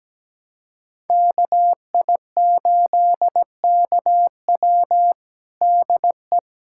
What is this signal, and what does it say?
Morse code, heard as a keyed tone, about 17 wpm, 700 Hz. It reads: KI8KW DE
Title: USA Callsign DE Callsign - 5 Characters - 17wpm